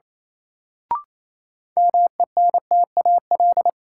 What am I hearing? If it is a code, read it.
Morse code, heard as a keyed tone, about 28 wpm, 700 Hz. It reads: MENTAL
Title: Sentences from Top 1000 Words - Review Part 4 - New Word Drill - 28wpm